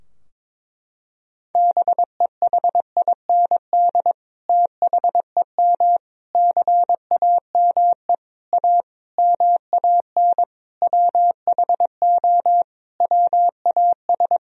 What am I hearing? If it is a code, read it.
Morse code, heard as a keyed tone, about 21 wpm, 700 Hz. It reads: BEHIND THEM CAME A MAN WHO WAS
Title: Sentences from Top 1000 Words - Review Part 4 - 22wpm